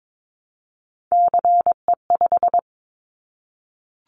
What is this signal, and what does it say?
Morse code, heard as a keyed tone, about 22 wpm, 700 Hz. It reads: CE5
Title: Callsign Prefixes for Top 100 DXCC - 22wpm